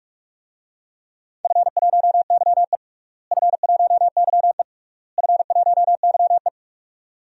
Morse code, heard as a keyed tone, about 41 wpm, 700 Hz. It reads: F1YE F1YE F1YE
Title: DXCC-100 Call Signs - 4 Characters - Repeated 3x - 45wpm